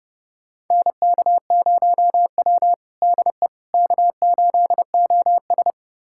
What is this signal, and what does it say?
Morse code, heard as a keyed tone, about 30 wpm, 700 Hz. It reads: NK0W DE K8OS
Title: USA Callsign DE Callsign - 4 Characters - 30wpm